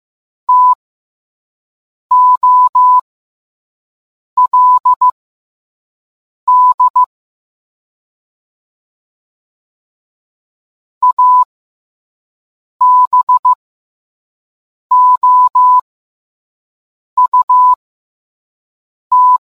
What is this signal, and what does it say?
Morse code, heard as a keyed tone, about 15 wpm, spaced as for 5 wpm, 1000 Hz. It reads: TOLD ABOUT